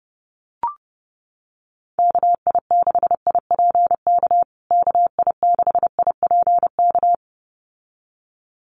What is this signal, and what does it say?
Morse code, heard as a keyed tone, about 30 wpm, 700 Hz. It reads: KI6IPK KI6IPK
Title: USA Call Signs Repeated 2x 30wpm